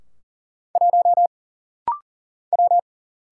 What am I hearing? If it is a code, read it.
Morse code, heard as a keyed tone, about 40 wpm, 700 Hz. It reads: 1 W
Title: Single Letter-Number - Rapid-Fire 40wpm